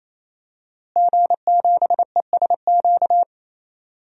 Morse code, heard as a keyed tone, about 28 wpm, 700 Hz. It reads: G7ESQ
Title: DXCC-100 Call Signs - 5 Characters - Repeated 3x - 28wpm